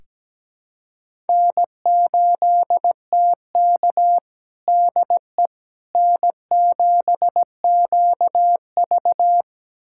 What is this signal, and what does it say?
Morse code, heard as a keyed tone, about 17 wpm, 700 Hz. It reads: N8TK DE N7QV
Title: USA Callsign DE Callsign - 4 Characters - 17wpm